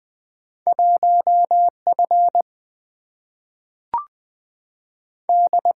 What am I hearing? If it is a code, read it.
Morse code, heard as a keyed tone, about 20 wpm, 700 Hz. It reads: 1F D
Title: Single-Letter Single-Number 20wpm